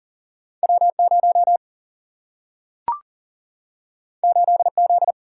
Morse code, heard as a keyed tone, about 39 wpm, 700 Hz. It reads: W0 8Z
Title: Single-Letter Single-Number 40wpm